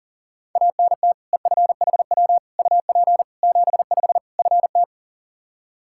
Morse code, heard as a keyed tone, about 40 wpm, 700 Hz. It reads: ANT EFHW UP 75 FT
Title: Practice QSO - Exchange 2 - Info - 40wpm